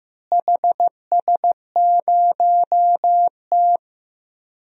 Morse code, heard as a keyed tone, about 15 wpm, 700 Hz. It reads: HS0T
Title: DXCC-100 Call Signs - 4 Characters - Repeated 3x - 15wpm